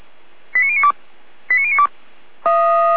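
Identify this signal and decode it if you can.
Five-tone selective call (ZVEI), heard as quick runs of five tones, tones of about 70 ms. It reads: 89092 89092
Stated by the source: ZVEI five-tone selective call received on an SDR.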